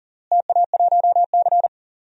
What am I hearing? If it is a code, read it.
Morse code, heard as a keyed tone, about 40 wpm, 700 Hz. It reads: TA1C
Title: DXCC-100 Call Signs - 4 Characters - Repeated 3x - 40wpm